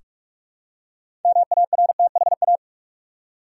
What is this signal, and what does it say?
Morse code, heard as a keyed tone, about 46 wpm, 700 Hz. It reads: MARTHA
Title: USA Common Names - 45wpm